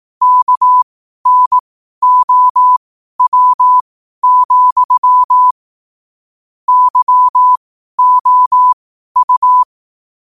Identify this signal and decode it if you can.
Morse code, heard as a keyed tone, about 18 wpm, spaced as for 12 wpm, 1000 Hz. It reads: KNOW, YOU
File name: qotdfiles / 12wpm